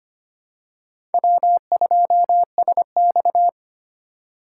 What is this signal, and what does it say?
Morse code, heard as a keyed tone, about 25 wpm, 700 Hz. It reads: W2SX
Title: USA Call Signs - 4 Characters - 4x Character Spacing - 25wpm